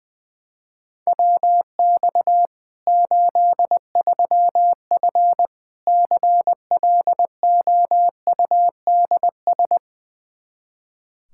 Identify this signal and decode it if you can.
Morse code, heard as a keyed tone, about 21 wpm, 700 Hz. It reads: WX 83F CLOUDS
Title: Practice QSO - Exchange 2 - Info - 20wpm